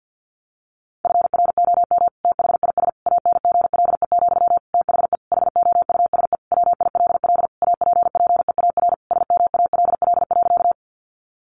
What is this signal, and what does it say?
Morse code, heard as a keyed tone, about 48 wpm, 700 Hz. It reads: FROM THIS ANGLE, THE HOUSE WILL APPEAR SMALL.
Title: Sentences from Top 1500 Words 50wpm